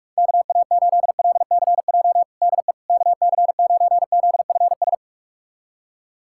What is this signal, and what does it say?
Morse code, heard as a keyed tone, about 44 wpm, 700 Hz. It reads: KA8LCJ DE KC9ZFS
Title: USA Callsign DE Callsign - 6 Characters - 45wpm